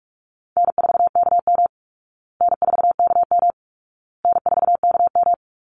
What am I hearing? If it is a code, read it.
Morse code, heard as a keyed tone, about 44 wpm, 700 Hz. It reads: N4KM N4KM N4KM
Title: Call Signs Repeated 2x - 45wpm